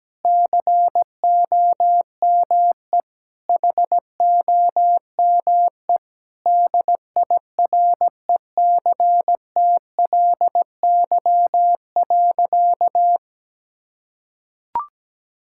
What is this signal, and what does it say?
Morse code, heard as a keyed tone, about 17 wpm, 700 Hz. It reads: COME HOME DIRECTLY.
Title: Sentences from Top 1000 Words - Review Part 5 - 17wpm